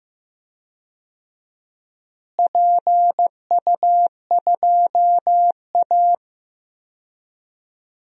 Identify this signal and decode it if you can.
Morse code, heard as a keyed tone, about 16 wpm, 700 Hz. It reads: PU2A
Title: DXCC-100 Call Signs - 4 Characters - Repeated 2x - 15wpm